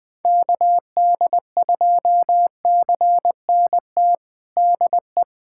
Morse code, heard as a keyed tone, about 20 wpm, 700 Hz. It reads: KD2CNT DE
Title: USA Callsign DE Callsign - 6 Characters - 20wpm